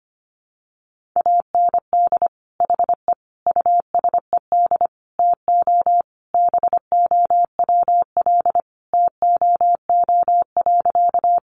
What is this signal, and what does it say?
Morse code, heard as a keyed tone, about 25 wpm, 700 Hz. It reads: AND HE USED TO BOWL TOO.
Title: Sentences from Top 1200 Words 25wpm